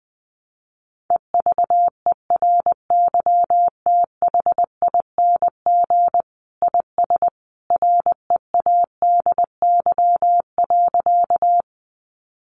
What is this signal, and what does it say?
Morse code, heard as a keyed tone, about 20 wpm, 700 Hz. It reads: EVERYTHING IS READY.
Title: Sentences from Top 700 Words 20wpm